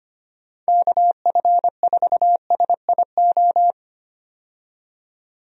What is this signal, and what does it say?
Morse code, heard as a keyed tone, about 25 wpm, 700 Hz. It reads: KF4SIO
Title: USA Call Signs - 6 Characters - 25wpm